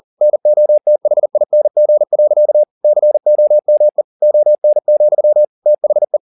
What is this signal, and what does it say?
Morse code, heard as a keyed tone, about 40 wpm, 600 Hz. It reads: NOTHING. COME ON, THE